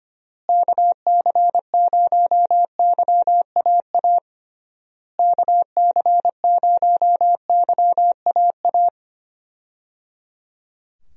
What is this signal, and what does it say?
Morse code, heard as a keyed tone, about 25 wpm, 700 Hz. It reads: KC0YAA KC0YAA
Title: USA Call Signs - 6 Characters - Repeated 3x - 25wpm